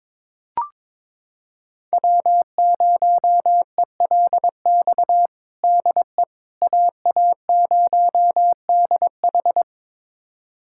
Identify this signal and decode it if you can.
Morse code, heard as a keyed tone, about 22 wpm, 700 Hz. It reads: W0ELX DE AA0DH
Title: USA Callsign DE Callsign - 5 Characters - 22wpm